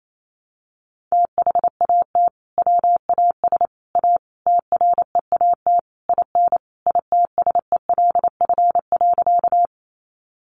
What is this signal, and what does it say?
Morse code, heard as a keyed tone, about 28 wpm, 700 Hz. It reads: THAT WAS A TREAT IN ITSELF.
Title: Sentences from Top 1300 Words 28wpm